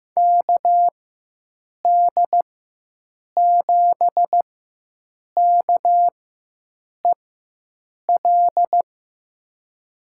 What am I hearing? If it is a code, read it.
Morse code, heard as a keyed tone, about 16 wpm, spaced as for 7 wpm, 700 Hz. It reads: KD7KEL
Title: USA Call Signs - 6 Characters - 4x Character Spacing - 15wpm